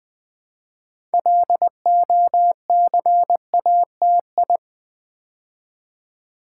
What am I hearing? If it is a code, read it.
Morse code, heard as a keyed tone, about 20 wpm, 700 Hz. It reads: LOCATI
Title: Build A Word - 2 Letters At A Time - 8-Letter Words - 20wpm